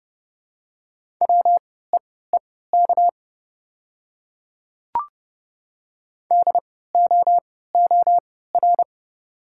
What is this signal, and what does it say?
Morse code, heard as a keyed tone, about 30 wpm, spaced as for 17 wpm, 700 Hz. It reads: WEEK DOOR
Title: 3x Character Spacing - Top 500 Words - 30wpm